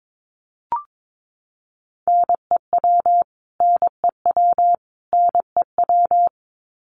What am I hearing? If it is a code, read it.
Morse code, heard as a keyed tone, about 22 wpm, 700 Hz. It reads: NEW NEW NEW
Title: Top 500 Words Repeated 22wpm